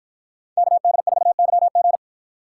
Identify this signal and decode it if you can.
Morse code, heard as a keyed tone, about 53 wpm, 700 Hz. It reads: KD4YG